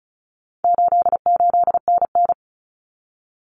Morse code, heard as a keyed tone, about 34 wpm, 700 Hz. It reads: 88NN
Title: Common Cut Numbers - 4-Digit 35wpm